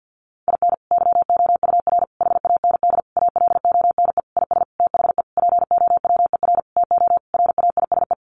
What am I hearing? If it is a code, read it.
Morse code, heard as a keyed tone, about 50 wpm, 700 Hz. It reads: IN YOUR HAND ALONE IS THE POWER TO RAISE
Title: Sentences from Top 1400 Words 50wpm